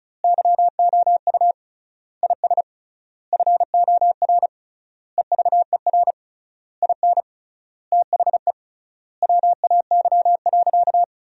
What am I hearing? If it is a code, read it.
Morse code, heard as a keyed tone, about 36 wpm, 700 Hz. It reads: YOU IS FOR EVER IN THE WAY.
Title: 3x Word-Spacing - Sentences from Top 400 Words - 35wpm